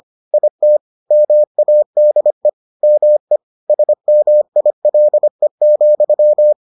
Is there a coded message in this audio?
Yes: Morse code from 0.3 to 6.5 s, heard as a keyed tone, about 25 wpm, 600 Hz. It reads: IT MADE ME SMILE,